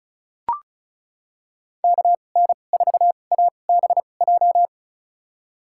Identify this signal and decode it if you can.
Morse code, heard as a keyed tone, about 34 wpm, spaced as for 25 wpm, 690 Hz. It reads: KN4ABJ
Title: USA Call Signs - 6 Characters - 2x Character Spacing - 35wpm